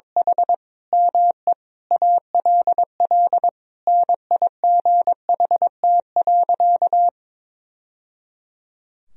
Morse code, heard as a keyed tone, about 22 wpm, 700 Hz. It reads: H ME ALL NIGHT.
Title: Sentences from Top 1000 Words - Review Part 2 - 22wpm